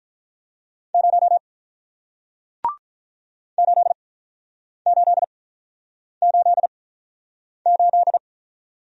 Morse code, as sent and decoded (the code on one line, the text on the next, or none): ----- / ---.. ---.. ---.. ---..
0 8888